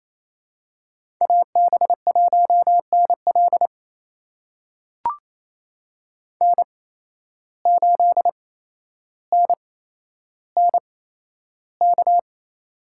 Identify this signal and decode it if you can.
Morse code, heard as a keyed tone, about 28 wpm, spaced as for 12 wpm, 700 Hz. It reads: AB1NL N8NNK
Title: USA Call Signs - 5 Characters - 8x Character Spacing - 28wpm